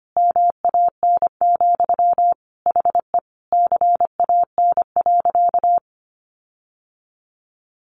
Morse code, heard as a keyed tone, about 25 wpm, 700 Hz. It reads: MAN, HE CAN.
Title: Sentences from Top 1500 Words 25wpm